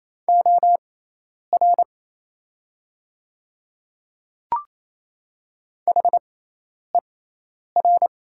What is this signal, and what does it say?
Morse code, heard as a keyed tone, about 27 wpm, spaced as for 10 wpm, 700 Hz. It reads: OR HER
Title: 6x Character Spacing - Top 100 Words - 28wpm